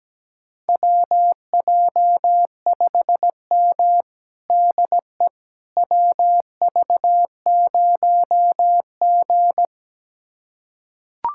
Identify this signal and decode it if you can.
Morse code, heard as a keyed tone, about 17 wpm, 700 Hz. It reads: WJ5M DE WV0G
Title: USA Callsign DE Callsign - 4 Characters - 17wpm